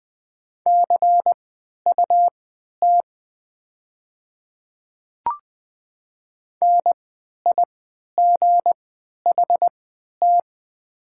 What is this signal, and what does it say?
Morse code, heard as a keyed tone, about 20 wpm, spaced as for 11 wpm, 700 Hz. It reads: CUT NIGHT
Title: 3x Character Spacing - Top 500 Words - 20wpm